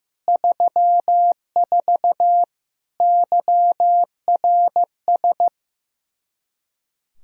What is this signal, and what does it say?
Morse code, heard as a keyed tone, about 15 wpm, 700 Hz. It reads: 34 YRS
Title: K1USN SST QSO Practice 15wpm